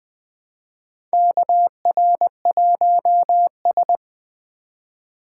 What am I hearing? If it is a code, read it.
Morse code, heard as a keyed tone, about 20 wpm, 700 Hz. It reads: KR1S
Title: USA Call Signs - 4 Characters - 4x Character Spacing - 20wpm